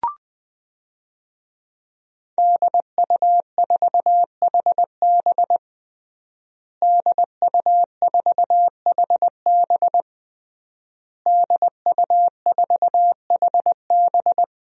Morse code, heard as a keyed tone, about 20 wpm, 700 Hz. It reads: DU4HB DU4HB DU4HB
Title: DXCC-100 Call Signs - 5 Characters - Repeated 3x - 20wpm